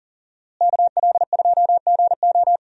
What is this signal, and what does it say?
Morse code, heard as a keyed tone, about 40 wpm, 700 Hz. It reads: KL2GO
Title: USA Call Signs - 5 Characters - Repeated 2x - 40wpm